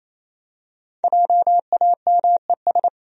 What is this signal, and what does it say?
Morse code, heard as a keyed tone, about 28 wpm, 700 Hz. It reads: JAMES
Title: USA Common Names - Max 5 Letters - 28wpm